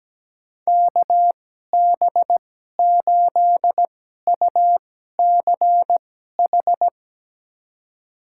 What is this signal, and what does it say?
Morse code, heard as a keyed tone, about 17 wpm, spaced as for 12 wpm, 700 Hz. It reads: KB8UCH
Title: USA Call Signs - 6 Characters - 2x Character Spacing - 17wpm